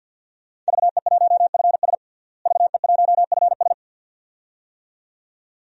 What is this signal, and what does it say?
Morse code, heard as a keyed tone, about 50 wpm, 700 Hz. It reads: VE1FS VE1FS
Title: DXCC-100 Call Signs - 5 Characters - Repeated 2x - 50wpm